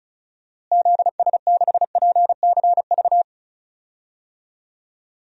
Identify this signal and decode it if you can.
Morse code, heard as a keyed tone, about 34 wpm, 700 Hz. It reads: ZS6PCV
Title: DXCC-100 Call Signs - 6 Characters - Repeated 2x - 35wpm